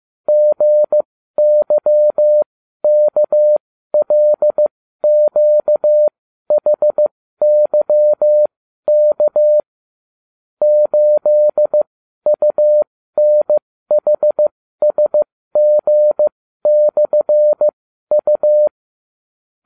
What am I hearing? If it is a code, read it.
Morse code, heard as a keyed tone, about 15 wpm, spaced as for 12 wpm, 600 Hz. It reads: GYKLQHYK 8UNHSG/U